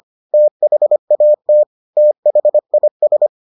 Morse code, heard as a keyed tone, about 25 wpm, 600 Hz. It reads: THAT THIS